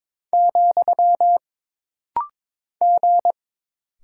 Morse code, heard as a keyed tone, about 22 wpm, 700 Hz. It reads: , G
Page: This rapid-fire timing is geared to push your ICR (Instant Character Recognition) skills.